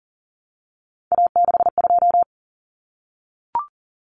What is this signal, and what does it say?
Morse code, heard as a keyed tone, about 40 wpm, 700 Hz. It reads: A62
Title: Common Cut Numbers - 3-Digit 40wpm